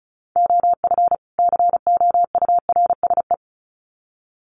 Morse code, heard as a keyed tone, about 34 wpm, 700 Hz. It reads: OF COURSE
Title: Sets of 2 Words 35wpm